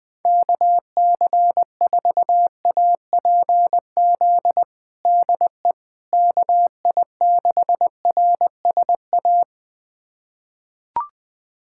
Morse code, heard as a keyed tone, about 20 wpm, 700 Hz. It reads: KC4APZ DE KI6RSA
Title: USA Callsign DE Callsign - 6 Characters - 20wpm